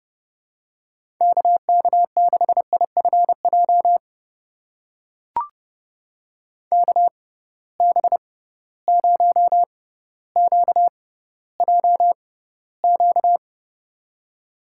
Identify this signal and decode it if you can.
Morse code, heard as a keyed tone, about 32 wpm, spaced as for 16 wpm, 700 Hz. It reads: KK6IFJ KB0QJQ
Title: USA Call Signs - 6x Character Spacing - 30wpm